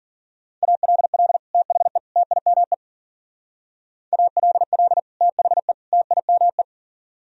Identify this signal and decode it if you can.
Morse code, heard as a keyed tone, about 50 wpm, 700 Hz. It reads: ALL THE TIME ALL THE TIME